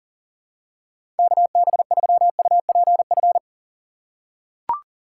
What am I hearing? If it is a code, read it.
Morse code, heard as a keyed tone, about 38 wpm, 700 Hz. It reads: KB3UPF
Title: USA Call Signs - 3x Character Spacing - 40wpm